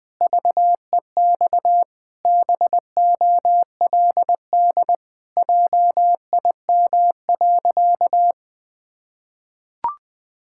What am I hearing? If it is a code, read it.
Morse code, heard as a keyed tone, about 20 wpm, 700 Hz. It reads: VEX BOLD JIM.
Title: Pangram Challenge v2 - 20wpm